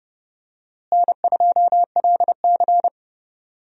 Morse code, heard as a keyed tone, about 30 wpm, 700 Hz. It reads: N2LC
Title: USA Call Signs - 4 Characters - Repeated 3x - 30wpm